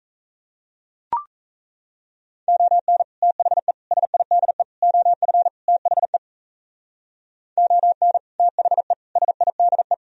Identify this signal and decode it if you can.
Morse code, heard as a keyed tone, about 34 wpm, 700 Hz. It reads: ON THE SIDE OF THE ON THE SIDE
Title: Speed-Racing - Sets of 5 Words 28wpm